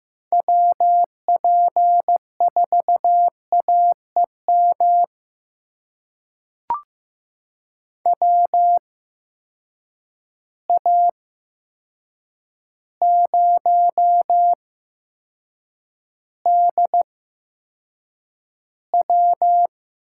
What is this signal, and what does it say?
Morse code, heard as a keyed tone, about 15 wpm, spaced as for 7 wpm, 700 Hz. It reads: WP4AEM WA0DW
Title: USA Call Signs - 6 Characters - 8x Character Spacing - 15wpm